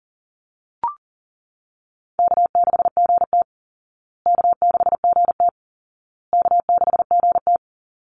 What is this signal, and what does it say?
Morse code, heard as a keyed tone, about 40 wpm, 700 Hz. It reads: K6GT K6GT K6GT
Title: Easy Call Signs Repeated 3x - 40wpm